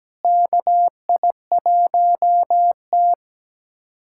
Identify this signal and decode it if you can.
Morse code, heard as a keyed tone, about 17 wpm, 700 Hz. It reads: KI1T
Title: USA Callsign DE Callsign - 4 Characters - 17wpm